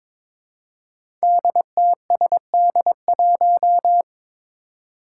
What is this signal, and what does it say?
Morse code, heard as a keyed tone, about 22 wpm, 700 Hz. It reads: DTSD1
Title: Five Challenging Letter-Number Combinations 22wpm